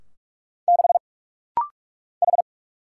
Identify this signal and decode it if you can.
Morse code, heard as a keyed tone, about 44 wpm, 700 Hz. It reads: 6 H
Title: Single Letter-Number - Rapid-Fire 45wpm